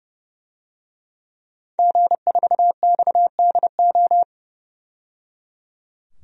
Morse code, heard as a keyed tone, about 30 wpm, 700 Hz. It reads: G4XDO
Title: DXCC-100 Call Signs - 5 Characters - 30wpm